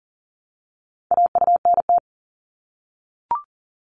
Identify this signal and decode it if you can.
Morse code, heard as a keyed tone, about 40 wpm, 700 Hz. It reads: AUNT
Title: Multi-letter Phonemes Repeated 3x 40wpm